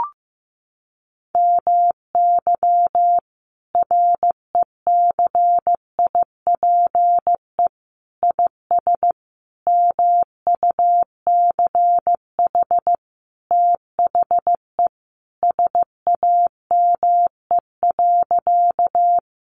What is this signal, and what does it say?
Morse code, heard as a keyed tone, about 15 wpm, 700 Hz. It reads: MY RECIPE IS MUCH THE SAME.
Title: Sentences from Top 1200 Words 15wpm